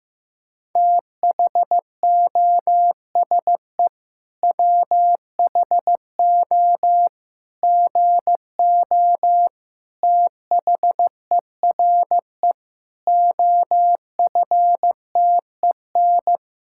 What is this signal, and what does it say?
Morse code, heard as a keyed tone, about 15 wpm, 700 Hz. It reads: THOSE WHO GO THERE OFTEN
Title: Sentences from Top 1000 Words - Review Part 5 - 15wpm